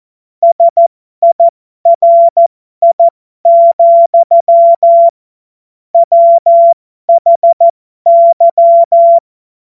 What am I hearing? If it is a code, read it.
Morse code, heard as a keyed tone, about 14 wpm, spaced as for 12 wpm, 680 Hz. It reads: SIRI, WHY
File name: Morsekurs / Witze